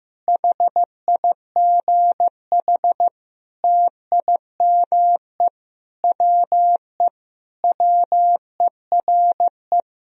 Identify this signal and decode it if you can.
Morse code, heard as a keyed tone, about 16 wpm, 700 Hz. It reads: HIGH TIME WE WERE
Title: Sentences from Top 1000 Words - Review Part 1 - 15wpm